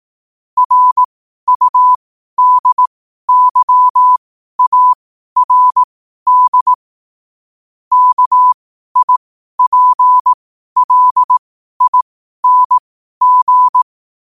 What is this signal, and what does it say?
Morse code, heard as a keyed tone, about 18 wpm, spaced as for 12 wpm, 1000 Hz. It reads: RUDYARD KIPLING
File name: qotdfiles / 12wpm